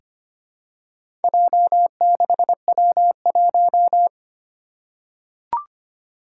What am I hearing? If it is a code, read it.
Morse code, heard as a keyed tone, about 25 wpm, 700 Hz. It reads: J6W1
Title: Four Challenging Letter-Number Combinations 25wpm